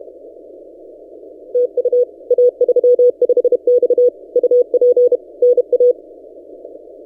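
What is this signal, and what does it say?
Morse code, heard as a keyed tone, about 30 wpm, 490 Hz. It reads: TU A35X UP NA
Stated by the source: Morse from DXpedition station A35X on 12 m, posted by N6ML.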